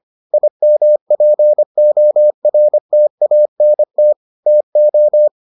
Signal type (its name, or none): Morse code